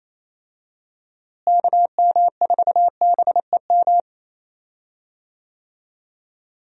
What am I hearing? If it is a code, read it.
Morse code, heard as a keyed tone, about 28 wpm, 700 Hz. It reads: KM4BEM
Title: USA Call Signs - 6 Characters - Repeated 2x - 28wpm